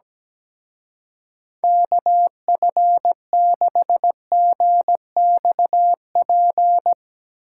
Morse code, heard as a keyed tone, about 18 wpm, 700 Hz. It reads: KF6GXP
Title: Speed Racing - USA Call Signs 17wpm